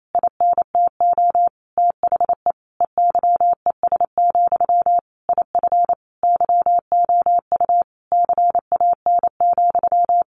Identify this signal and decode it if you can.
Morse code, heard as a keyed tone, about 28 wpm, 700 Hz. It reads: INTO THE EYES, IF YOU CAN,
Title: A Sound Of Thunder 28wpm